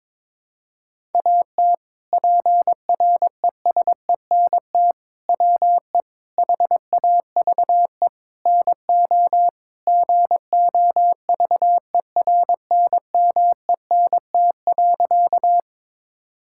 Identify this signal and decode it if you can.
Morse code, heard as a keyed tone, about 22 wpm, 700 Hz. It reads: AT PRESENT WE HAVE NO GOVERNMENT.
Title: Sentences from Top 1000 Words - Review Part 5 - 22wpm